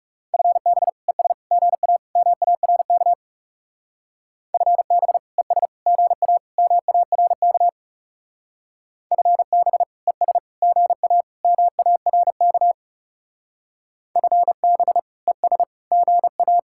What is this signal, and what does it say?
Morse code, heard as a keyed tone, about 36 wpm, 700 Hz. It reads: FB ES GA MARK FB ES GA MARK FB ES GA MARK FB ES GA
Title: Speed-Racing Common QSO Phrases - 30wpm